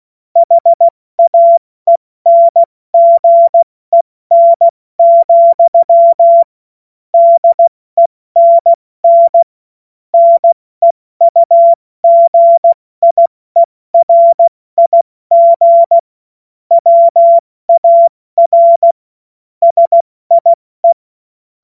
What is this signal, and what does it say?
Morse code, heard as a keyed tone, about 16 wpm, spaced as for 14 wpm, 680 Hz. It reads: HAENGEN, DENN NEUGIERIG WAR SIE
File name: Morsekurs / Witze / Witz 17.Oktober 2023 70 F80